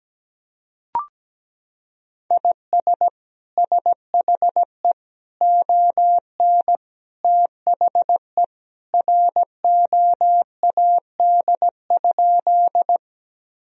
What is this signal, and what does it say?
Morse code, heard as a keyed tone, about 17 wpm, 700 Hz. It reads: IS SHE ON THE ROAD?
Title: Sentences from Top 1000 Words - Review Part 2 - 17wpm